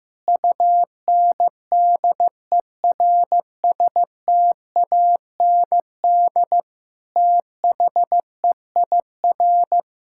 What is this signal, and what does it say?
Morse code, heard as a keyed tone, about 15 wpm, 700 Hz. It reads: UNDERSTAND THEIR
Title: Sentences from Top 1000 Words - Review Part 6 - 15wpm